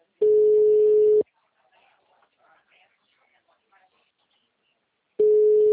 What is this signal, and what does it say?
Ringback tone (Europe, 425 Hz), with speech.